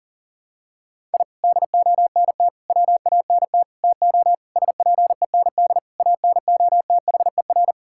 Morse code, heard as a keyed tone, about 40 wpm, 690 Hz. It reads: I DONT WANT TO SPEND ANOTHER
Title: Sentences from Top 1000 Words - Review Part 5 - 40wpm